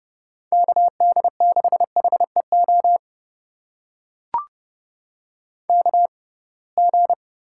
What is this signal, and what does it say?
Morse code, heard as a keyed tone, about 30 wpm, spaced as for 23 wpm, 700 Hz. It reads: KD6HEO KG
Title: USA Call Signs - 6 Characters - 6x Character Spacing - 30wpm